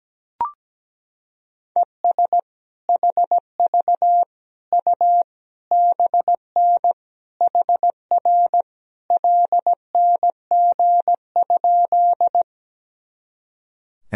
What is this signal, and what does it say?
Morse code, heard as a keyed tone, about 17 wpm, 700 Hz. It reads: ES HV U BN HR LNG?
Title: Sentences from Top 200 Words and Abbreviations 17wpm